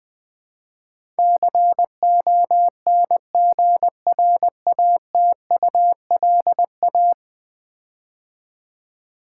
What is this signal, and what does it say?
Morse code, heard as a keyed tone, about 20 wpm, 700 Hz. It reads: CONGRATULA
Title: Build A Word - 5 Letters At A Time - 12-Letter Words - 20wpm